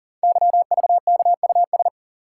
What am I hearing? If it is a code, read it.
Morse code, heard as a keyed tone, about 39 wpm, 700 Hz. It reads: YVKUS
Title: Code Groups - 5 Letters - 40wpm